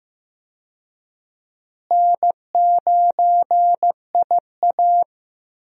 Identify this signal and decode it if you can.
Morse code, heard as a keyed tone, about 15 wpm, 700 Hz. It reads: N9IA
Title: USA Call Signs - 4 Characters - Repeated 2x - 15wpm